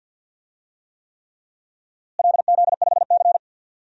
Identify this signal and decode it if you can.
Morse code, heard as a keyed tone, about 51 wpm, 700 Hz. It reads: LZ5C
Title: DXCC-100 Call Signs - 4 Characters - Repeated 3x - 50wpm